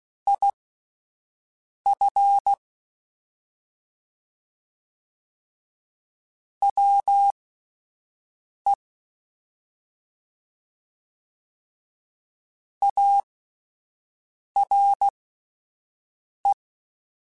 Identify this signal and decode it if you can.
Morse code, heard as a keyed tone, about 16 wpm, spaced as for 5 wpm, 800 Hz. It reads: IF WE ARE
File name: (spaced) / QSOs Spaced-mp3s / WA7QOX 1 @10 sp